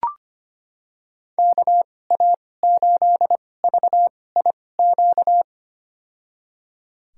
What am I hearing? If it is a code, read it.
Morse code, heard as a keyed tone, about 25 wpm, spaced as for 18 wpm, 700 Hz. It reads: KA8VIQ
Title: USA Call Signs - 6 Characters - 2x Character Spacing - 25wpm